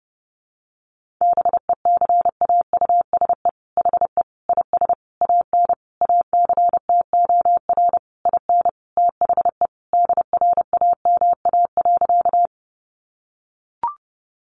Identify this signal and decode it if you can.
Morse code, heard as a keyed tone, about 30 wpm, 700 Hz. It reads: BECAUSE HE IS AN ACTOR IN THE DRAMA.
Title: Sentences from Top 1500 Words 30wpm